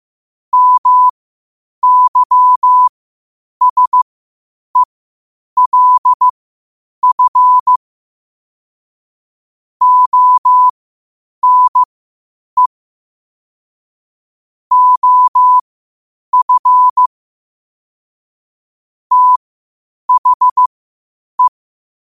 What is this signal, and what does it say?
Morse code, heard as a keyed tone, about 15 wpm, spaced as for 8 wpm, 1000 Hz. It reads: MYSELF ONE OF THE
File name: qotdfiles / 8wpm